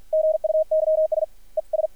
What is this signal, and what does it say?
Morse code, heard as a keyed tone, about 53 wpm, 620 Hz, with radio noise. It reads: GUYS ES
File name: archived / CRSnet